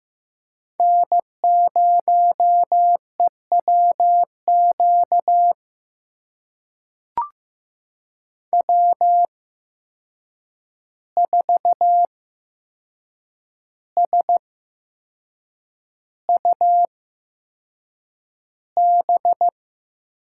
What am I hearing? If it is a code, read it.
Morse code, heard as a keyed tone, about 15 wpm, spaced as for 6 wpm, 700 Hz. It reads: N0EWQ W4SUB
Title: USA Call Signs - 5 Characters - 8x Character Spacing - 15wpm